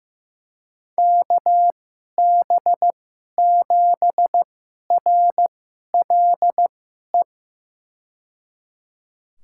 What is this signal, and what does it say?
Morse code, heard as a keyed tone, about 15 wpm, spaced as for 11 wpm, 700 Hz. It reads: KB7RLE
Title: USA Call Signs - 6 Characters - 2x Character Spacing - 15wpm